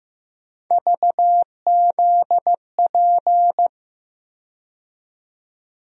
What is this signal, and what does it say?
Morse code, heard as a keyed tone, about 15 wpm, 700 Hz. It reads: VZP
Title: USA Call Sign Postfixes - 3 Letters 15wpm